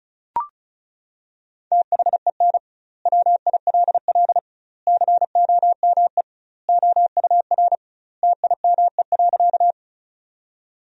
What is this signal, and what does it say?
Morse code, heard as a keyed tone, about 36 wpm, 700 Hz. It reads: THEN WILL COME OUR TIME.
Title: 2x Word-Spacing - Sentences from Top 200 Words - 35wpm